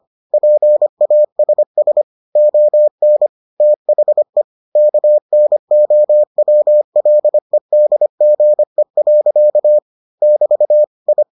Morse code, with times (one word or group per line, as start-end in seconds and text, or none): 0.3-2.0 PASS
2.4-3.3 ON
3.6-4.4 THE
4.8-9.8 KNOWLEDGE.
10.2-10.8 <BT>
11.1-11.2 I